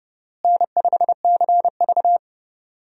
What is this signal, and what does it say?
Morse code, heard as a keyed tone, about 30 wpm, 700 Hz. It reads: N5CV
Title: USA Call Signs - 4 Characters - Repeated 3x - 30wpm